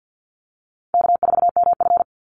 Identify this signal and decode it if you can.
Morse code, heard as a keyed tone, about 48 wpm, 700 Hz. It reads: K4MF
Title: NA SOTA Chaser Callsigns 50wpm